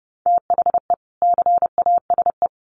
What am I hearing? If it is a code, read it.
Morse code, heard as a keyed tone, about 28 wpm, 700 Hz. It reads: THE CASE
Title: Sets of 2 Words 30wpm